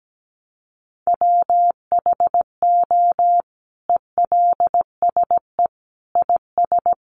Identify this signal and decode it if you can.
Morse code, heard as a keyed tone, about 17 wpm, 700 Hz. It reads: WHO ELSE IS
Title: Sentences from Top 800 Words 17wpm